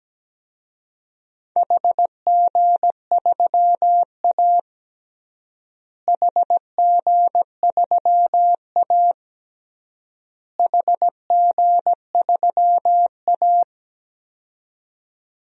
Morse code, heard as a keyed tone, about 17 wpm, 700 Hz. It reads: HG3A HG3A HG3A
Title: DXCC-100 Call Signs - 4 Characters - Repeated 3x - 17wpm